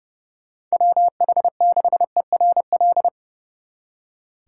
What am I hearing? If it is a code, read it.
Morse code, heard as a keyed tone, about 38 wpm, 700 Hz. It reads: WH6ERL